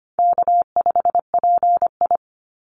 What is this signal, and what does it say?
Morse code, heard as a keyed tone, about 25 wpm, 700 Hz. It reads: K5PI
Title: Easy Call Signs Repeated 2x - 25wpm